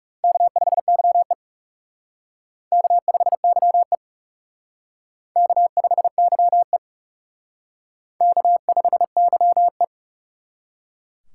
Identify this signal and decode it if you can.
Morse code, heard as a keyed tone, about 36 wpm, 700 Hz. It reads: K5YE K5YE K5YE K5YE
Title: Speed Racing - USA 1x2 Call Signs 30wpm